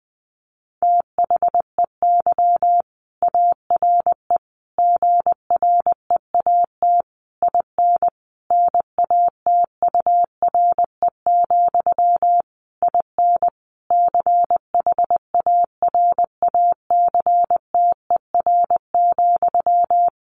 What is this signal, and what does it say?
Morse code, heard as a keyed tone, about 20 wpm, 700 Hz. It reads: THEY ARE GREAT IN NATURE, IN CHARACTER,
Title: Sentences from Top 1400 Words 20wpm